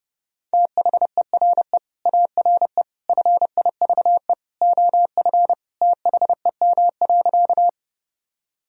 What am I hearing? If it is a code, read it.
Morse code, heard as a keyed tone, about 32 wpm, 700 Hz. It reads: THERE ARE FIVE OF THEM.
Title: Sentences from Top 1000 Words - Review Part 2 - 30wpm